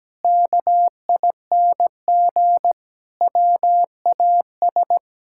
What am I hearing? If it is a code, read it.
Morse code, heard as a keyed tone, about 17 wpm, 700 Hz. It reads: KING WAS
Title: Sentences from Top 1000 Words - Review Part 6 - 17wpm